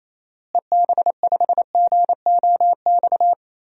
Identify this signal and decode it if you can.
Morse code, heard as a keyed tone, about 27 wpm, 700 Hz. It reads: EB5GOX
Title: DXCC-100 Call Signs - 6 Characters - Repeated 3x - 28wpm